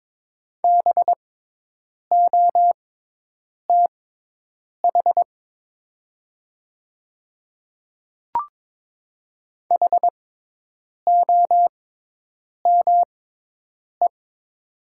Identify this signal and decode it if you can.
Morse code, heard as a keyed tone, about 22 wpm, spaced as for 8 wpm, 700 Hz. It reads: BOTH HOME
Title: 6x Character Spacing - Top 300 Words - 22wpm